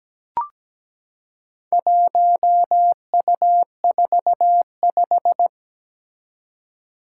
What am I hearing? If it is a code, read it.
Morse code, heard as a keyed tone, about 17 wpm, 700 Hz. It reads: 1U45
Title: Four Challenging Letter-Number Combinations 17wpm